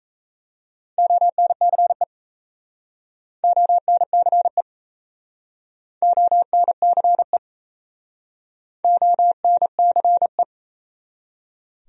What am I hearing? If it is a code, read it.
Morse code, heard as a keyed tone, about 36 wpm, 700 Hz. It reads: ONCE ONCE ONCE ONCE